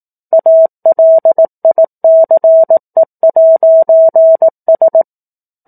Morse code, heard as a keyed tone, about 18 wpm, 650 Hz. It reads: ALICE'S